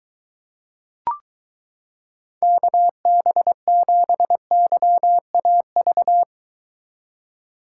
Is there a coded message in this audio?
Yes: Morse code from 2.4 to 6.2 s, heard as a keyed tone, about 23 wpm, 700 Hz. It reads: KB7YAV